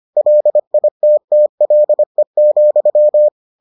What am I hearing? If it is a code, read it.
Morse code, heard as a keyed tone, about 25 wpm, 600 Hz. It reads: LITTLE,